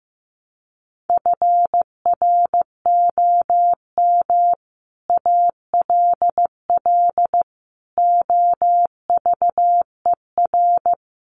Morse code, heard as a keyed tone, about 15 wpm, 700 Hz. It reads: FROM ALL OVER
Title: Sets of 5 Words 15wpm